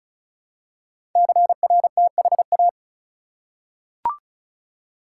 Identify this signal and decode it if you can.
Morse code, heard as a keyed tone, about 36 wpm, 700 Hz. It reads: CRTHA
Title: Code Groups - 5 Letters - 35wpm